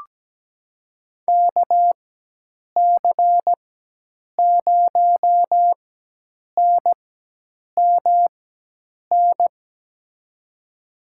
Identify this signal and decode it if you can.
Morse code, heard as a keyed tone, about 17 wpm, spaced as for 8 wpm, 700 Hz. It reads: KC0NMN
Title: USA Call Signs - 6 Characters - 4x Character Spacing - 17wpm